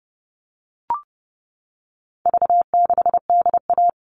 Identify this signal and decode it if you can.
Morse code, heard as a keyed tone, about 30 wpm, 700 Hz. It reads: V6DA
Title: All Cut Numbers - 4-Digit 30wpm